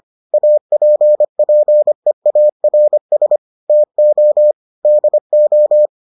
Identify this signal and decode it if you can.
Morse code, heard as a keyed tone, about 25 wpm, 600 Hz. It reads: APPEARS TO DO